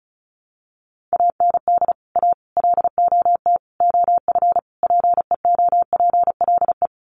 Morse code, heard as a keyed tone, about 34 wpm, 700 Hz. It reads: AND A LOT OF PEOPLE
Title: Sets of 5 Words 35wpm